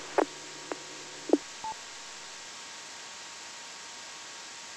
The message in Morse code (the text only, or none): none